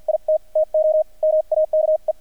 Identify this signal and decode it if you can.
Morse code, heard as a keyed tone, about 48 wpm, spaced as for 41 wpm, 620 Hz, with radio noise. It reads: IT TO MAKE
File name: archived / CRSnet